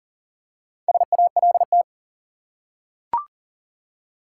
Morse code, heard as a keyed tone, about 40 wpm, 700 Hz. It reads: SALT
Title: Four Letter Words - 40wpm